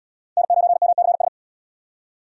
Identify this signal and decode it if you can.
Morse code, heard as a keyed tone, about 73 wpm, 700 Hz. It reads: N9MCS